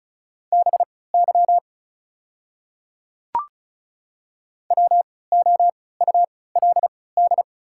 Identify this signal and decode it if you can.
Morse code, heard as a keyed tone, about 36 wpm, spaced as for 20 wpm, 690 Hz. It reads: BY WOULD
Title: 3x Character Spacing - Top 100 Words - 35wpm